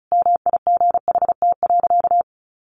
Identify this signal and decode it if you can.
Morse code, heard as a keyed tone, about 34 wpm, 700 Hz. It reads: MIGHT.
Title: Sentences from Top 500 Words 35wpm